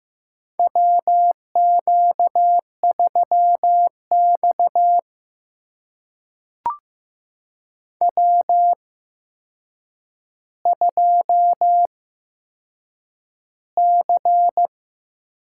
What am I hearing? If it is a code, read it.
Morse code, heard as a keyed tone, about 15 wpm, spaced as for 7 wpm, 700 Hz. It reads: WQ3X W2C
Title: USA Call Signs - 4 Characters - 8x Character Spacing - 15wpm